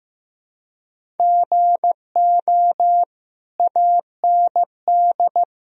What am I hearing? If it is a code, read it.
Morse code, heard as a keyed tone, about 16 wpm, 700 Hz. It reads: GO AND
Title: Sentences from Top 1000 Words - Review Part 3 - 15wpm